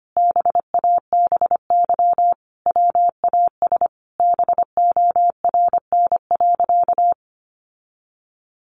Morse code, heard as a keyed tone, about 25 wpm, 700 Hz. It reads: BABY WAS BORN.
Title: Sentences from Top 1300 Words 25wpm